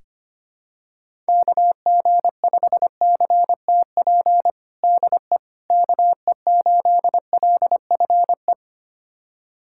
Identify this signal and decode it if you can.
Morse code, heard as a keyed tone, about 25 wpm, 700 Hz. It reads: KG5CTP DE KE8LFE
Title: USA Callsign DE Callsign - 6 Characters - 25wpm